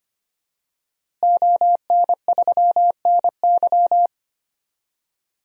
Morse code, heard as a keyed tone, about 25 wpm, 700 Hz. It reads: ON3NY
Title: DXCC-100 Call Signs - 5 Characters - Repeated 3x - 25wpm